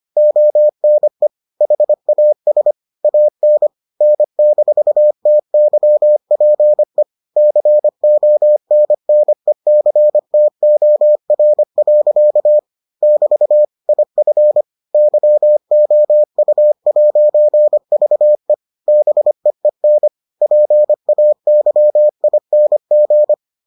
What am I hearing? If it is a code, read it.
Morse code, heard as a keyed tone, about 25 wpm, 600 Hz. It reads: ONE HAS AN N-TYPE CONNECTOR. <BT> IF YOU'VE BEEN PAYING